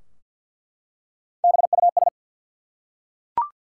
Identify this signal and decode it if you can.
Morse code, heard as a keyed tone, about 48 wpm, 700 Hz. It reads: BUS
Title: Three Letter Words - 50wpm